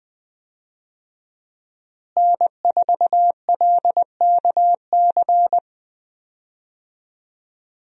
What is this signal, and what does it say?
Morse code, heard as a keyed tone, about 20 wpm, 700 Hz. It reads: N4LKC
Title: USA Call Signs - 5 Characters - Repeated 2x - 20wpm